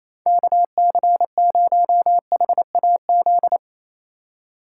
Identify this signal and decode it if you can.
Morse code, heard as a keyed tone, about 34 wpm, 700 Hz. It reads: KC0HAZ
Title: Speed Racing - USA 2x3 Call Signs 28wpm